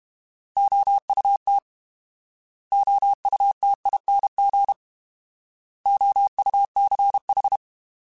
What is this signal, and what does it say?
Morse code, heard as a keyed tone, about 32 wpm, 800 Hz. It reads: OUT OUTING OUCH
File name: Alphabetical / Alphabetical mp3s / Os @30